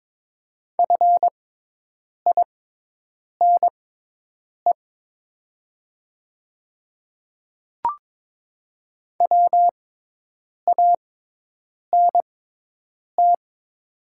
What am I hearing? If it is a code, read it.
Morse code, heard as a keyed tone, about 22 wpm, spaced as for 8 wpm, 700 Hz. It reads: FINE WANT
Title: 6x Character Spacing - Top 500 Words - 22wpm